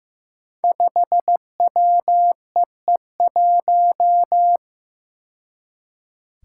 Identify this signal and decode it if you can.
Morse code, heard as a keyed tone, about 15 wpm, 700 Hz. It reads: 5WEE1
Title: Five Challenging Letter-Number Combinations 15wpm